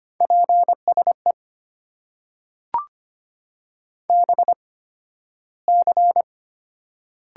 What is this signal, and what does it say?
Morse code, heard as a keyed tone, about 25 wpm, spaced as for 13 wpm, 700 Hz. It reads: PSE BC
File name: Top 90 QSO Elements - 8x Character Spacing - 25wpm